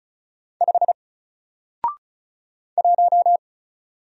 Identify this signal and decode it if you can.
Morse code, heard as a keyed tone, about 34 wpm, 700 Hz. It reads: 5 1
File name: Single-Letter-Number-35wpm